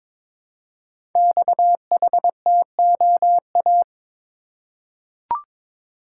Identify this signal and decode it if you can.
Morse code, heard as a keyed tone, about 22 wpm, 690 Hz. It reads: XHTOA
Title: Code Groups - 5 Letters - 22wpm